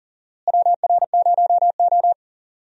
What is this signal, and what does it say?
Morse code, heard as a keyed tone, about 40 wpm, 700 Hz. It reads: WR0O
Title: USA Call Signs - 4 Characters - Repeated 3x - 40wpm